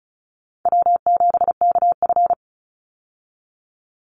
Morse code, heard as a keyed tone, about 34 wpm, 700 Hz. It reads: W7KF
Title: Easy Call Signs Repeated 3x - 35wpm